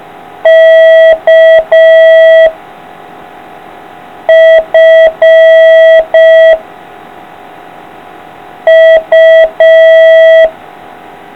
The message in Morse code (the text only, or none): K F U